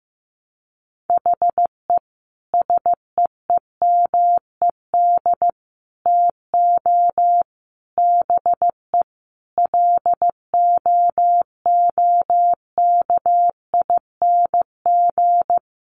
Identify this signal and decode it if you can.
Morse code, heard as a keyed tone, about 14 wpm, 700 Hz. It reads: HE SEEMED TO BE LOOKING
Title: A Sound Of Thunder 15wpm